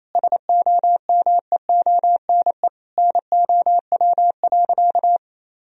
Morse code, heard as a keyed tone, about 28 wpm, 700 Hz. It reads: SOMEONE NOW.
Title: Sentences from Top 1000 Words - Review Part 4 - 28wpm